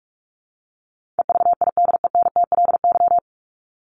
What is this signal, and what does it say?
Morse code, heard as a keyed tone, about 44 wpm, 700 Hz. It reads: EVIDENTLY
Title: LY Suffix - Top 100 Words with Base Words 45wpm